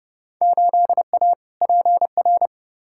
Morse code, heard as a keyed tone, about 30 wpm, 700 Hz. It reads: 8A PR
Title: Field Day Exchanges Encore - 30wpm